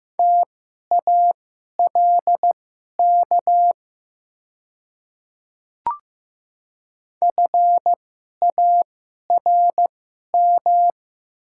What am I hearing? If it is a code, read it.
Morse code, heard as a keyed tone, about 15 wpm, spaced as for 11 wpm, 700 Hz. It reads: TALK FARM
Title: 2x Character Spacing - Top 500 Words - 15wpm